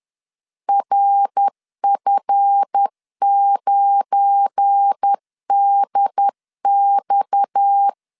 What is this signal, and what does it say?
Morse code, heard as a keyed tone, about 10 wpm, 800 Hz. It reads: RF9DX
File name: LEZIONI / 8 YJ45 / 8 YJ45 10wpm 2549